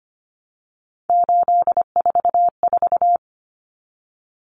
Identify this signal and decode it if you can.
Morse code, heard as a keyed tone, about 25 wpm, 700 Hz. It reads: 844
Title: Common Cut Numbers - 3-Digit 25wpm